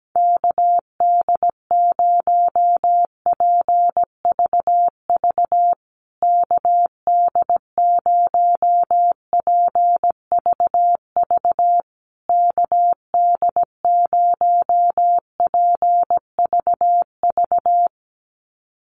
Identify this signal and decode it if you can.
Morse code, heard as a keyed tone, about 17 wpm, 700 Hz. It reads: KD0PVV KD0PVV KD0PVV
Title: USA Call Signs Repeated 3x 17wpm